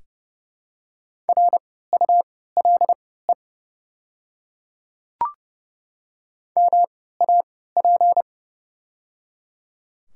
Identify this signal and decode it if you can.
Morse code, heard as a keyed tone, about 30 wpm, spaced as for 17 wpm, 700 Hz. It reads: RULE MAP
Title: 3x Character Spacing - Top 500 Words - 30wpm